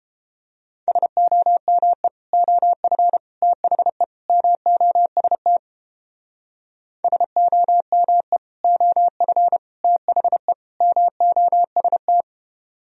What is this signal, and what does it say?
Morse code, heard as a keyed tone, about 32 wpm, 700 Hz. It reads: SOME OF THE MOST SOME OF THE MOST